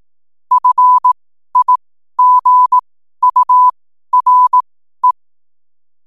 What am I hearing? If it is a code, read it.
Morse code, heard as a keyed tone, about 18 wpm, spaced as for 12 wpm, 1000 Hz. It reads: FIGURE
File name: n7net / 12wpm